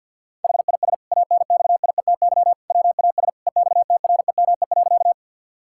Morse code, heard as a keyed tone, about 50 wpm, 700 Hz. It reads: HIS ANXIETY WAS EXTREME.
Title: Sentences from Top 1600 Words 50wpm